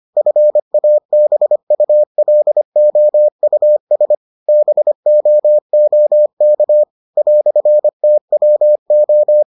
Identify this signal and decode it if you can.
Morse code, heard as a keyed tone, about 25 wpm, 600 Hz. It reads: FABULOUS BOOK "TWO